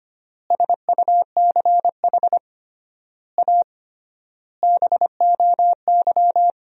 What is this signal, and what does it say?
Morse code, heard as a keyed tone, about 25 wpm, 700 Hz. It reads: SUCH A BOY
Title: 3x Word-Spacing - Sentences from Top 400 Words - 25wpm